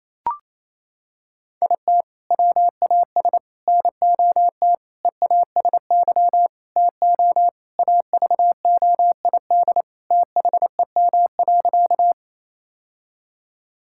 Morse code, heard as a keyed tone, about 28 wpm, 700 Hz. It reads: IT WAS NOT EASY TO AVOID THEM.
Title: Sentences from Top 1000 Words - Review Part 6 - 28wpm